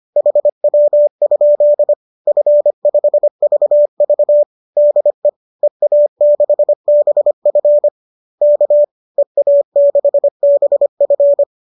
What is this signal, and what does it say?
Morse code, heard as a keyed tone, about 25 wpm, 570 Hz. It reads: HW? F5VV DE EA6BF K EA6BF